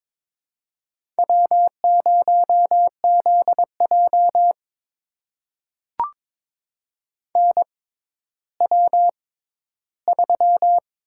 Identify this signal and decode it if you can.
Morse code, heard as a keyed tone, about 23 wpm, spaced as for 13 wpm, 700 Hz. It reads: W0ZJ NW3
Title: USA Call Signs - 4 Characters - 6x Character Spacing - 22wpm